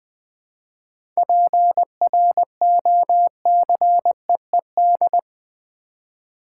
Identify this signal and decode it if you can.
Morse code, heard as a keyed tone, about 20 wpm, 700 Hz. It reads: PROCEED
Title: 7 Letter Words - First 3 Letters Given - 20wpm